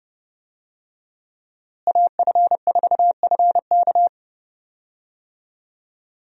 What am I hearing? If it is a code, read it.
Morse code, heard as a keyed tone, about 30 wpm, 700 Hz. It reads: AF4FK
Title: USA Call Signs - 5 Characters - Repeated 3x - 30wpm